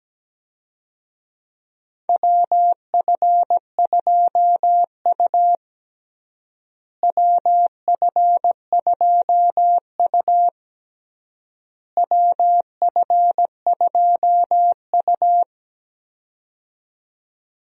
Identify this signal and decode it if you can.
Morse code, heard as a keyed tone, about 17 wpm, 700 Hz. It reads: WF2U WF2U WF2U
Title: USA Call Signs - 4 Characters - Repeated 3x - 17wpm